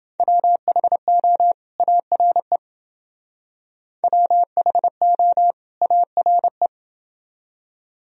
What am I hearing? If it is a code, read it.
Morse code, heard as a keyed tone, about 28 wpm, 700 Hz. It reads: WHO ARE WHO ARE